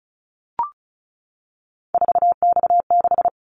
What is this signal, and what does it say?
Morse code, heard as a keyed tone, about 34 wpm, 700 Hz. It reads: 4X6
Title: Callsign Prefixes for Top 50 DXCC - 35wpm